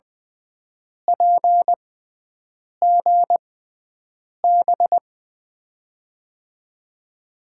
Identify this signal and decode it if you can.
Morse code, heard as a keyed tone, about 21 wpm, spaced as for 7 wpm, 700 Hz. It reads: PGB
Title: USA Call Signs - 5 Characters - 6x Character Spacing - 20wpm